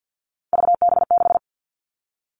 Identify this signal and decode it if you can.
Morse code, heard as a keyed tone, about 50 wpm, 700 Hz. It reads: VB6
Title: Three Challenging Letter-Number Combinations 50wpm